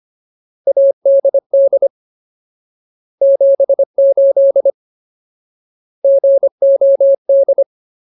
Morse code, heard as a keyed tone, about 25 wpm, 550 Hz. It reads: ADD 78 GOD
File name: Ultimate Word List / 25WPM Ultimate